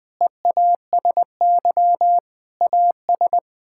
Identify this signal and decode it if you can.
Morse code, heard as a keyed tone, about 20 wpm, 700 Hz. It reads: EASY AS
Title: Sets of 8 Words - 20wpm